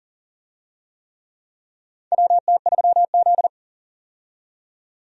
Morse code, heard as a keyed tone, about 37 wpm, 700 Hz. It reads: WT3Z
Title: USA Call Signs - 4 Characters - 40wpm